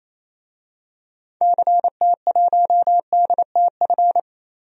Morse code, heard as a keyed tone, about 28 wpm, 700 Hz. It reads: CT1DTF
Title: DXCC-100 Call Signs - 6 Characters - Repeated 2x - 28wpm